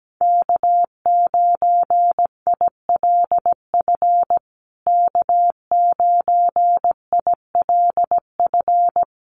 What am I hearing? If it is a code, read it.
Morse code, heard as a keyed tone, about 17 wpm, 700 Hz. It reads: K9ILF K9ILF
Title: USA Call Signs Repeated 3x 17wpm